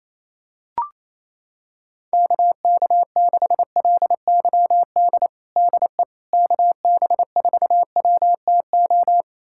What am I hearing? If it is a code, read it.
Morse code, heard as a keyed tone, about 28 wpm, 700 Hz. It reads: KK6LYD DE KB4WTO
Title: USA Callsign DE Callsign - 6 Characters - 28wpm